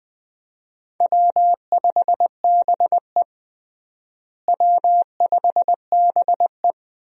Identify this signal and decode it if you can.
Morse code, heard as a keyed tone, about 20 wpm, 700 Hz. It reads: W5BE W5BE
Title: USA Call Signs - 4 Characters - Repeated 2x - 20wpm